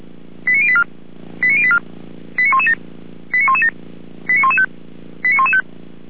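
Five-tone selective call (ZVEI), heard as quick runs of five tones, tones of about 70 ms. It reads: 89084 89084 89117 89117 89115 89115
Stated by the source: ZVEI five-tone call received by SDR.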